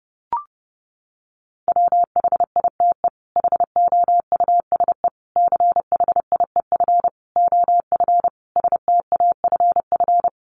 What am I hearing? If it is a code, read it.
Morse code, heard as a keyed tone, about 30 wpm, 700 Hz. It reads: WHITE HOUSE CHIEF OF STAFF
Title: Sets of 5 Words Encore 30wpm